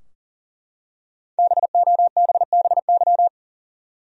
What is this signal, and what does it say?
Morse code, heard as a keyed tone, about 40 wpm, 700 Hz. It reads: BOBBY
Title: USA Common Names - 40wpm